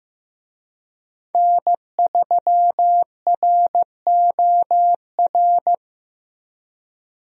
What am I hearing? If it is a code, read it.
Morse code, heard as a keyed tone, about 16 wpm, 700 Hz. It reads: N3ROR
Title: USA Call Signs - 5 Characters - Repeated 2x - 15wpm